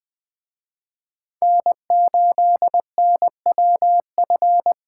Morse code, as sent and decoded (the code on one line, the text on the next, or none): -. ---.. -. .-- ..-.
N8NWF